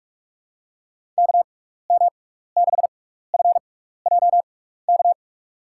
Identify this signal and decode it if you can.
Morse code, heard as a keyed tone, about 44 wpm, spaced as for 15 wpm, 700 Hz. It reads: KM6FJK
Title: USA Call Signs - 6x Character Spacing - 45wpm